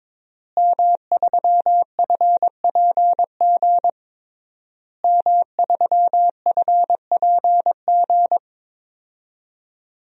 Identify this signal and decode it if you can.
Morse code, heard as a keyed tone, about 22 wpm, 700 Hz. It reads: M3FPG M3FPG
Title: DXCC-100 Call Signs - 5 Characters - Repeated 3x - 22wpm